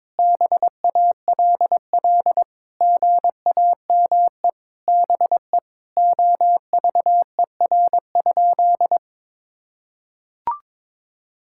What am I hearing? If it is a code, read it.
Morse code, heard as a keyed tone, about 21 wpm, 700 Hz. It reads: BALL GAME BE OVER?
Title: Sentences from Top 1000 Words - Review Part 6 - 22wpm